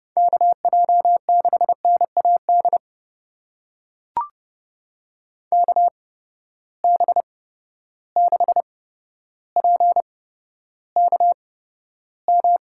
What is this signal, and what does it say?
Morse code, heard as a keyed tone, about 29 wpm, spaced as for 13 wpm, 700 Hz. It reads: KJ6NAD KB6PKM
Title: USA Call Signs - 6 Characters - 8x Character Spacing - 30wpm